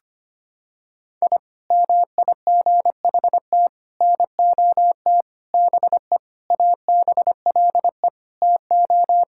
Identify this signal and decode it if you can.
Morse code, heard as a keyed tone, about 25 wpm, 700 Hz. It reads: I MIGHT NOT BE ABLE TO
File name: practice / 25wpm / Sentences with 4-Word Phrases - 25wpm